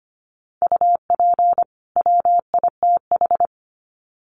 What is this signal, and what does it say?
Morse code, heard as a keyed tone, about 25 wpm, 700 Hz. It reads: UP WITH
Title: Sets of 2 Words 25wpm